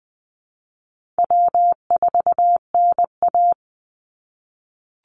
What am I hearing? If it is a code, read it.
Morse code, heard as a keyed tone, about 20 wpm, 700 Hz. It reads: W4NA
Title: Easy Call Signs - 20wpm